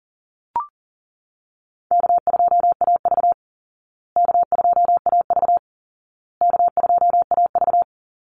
Morse code, heard as a keyed tone, about 40 wpm, 700 Hz. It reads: K2AV K2AV K2AV
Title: Easy Call Signs Repeated 3x - 40wpm